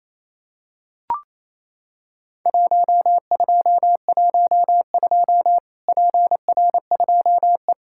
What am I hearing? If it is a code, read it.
Morse code, heard as a keyed tone, about 27 wpm, 700 Hz. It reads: 1212 PR2E
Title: Worked All Europe DX Contest - QTC Exchange - 28wpm